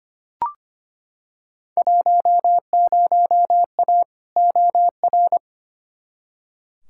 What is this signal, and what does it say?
Morse code, heard as a keyed tone, about 25 wpm, 700 Hz. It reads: 10A OR
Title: Field Day Exchanges - 25wpm